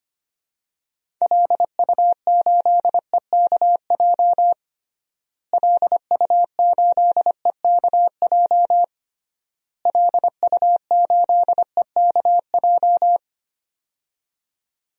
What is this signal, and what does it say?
Morse code, heard as a keyed tone, about 25 wpm, 700 Hz. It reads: LU8EKJ LU8EKJ LU8EKJ
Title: DXCC-100 Call Signs - 6 Characters - Repeated 3x - 25wpm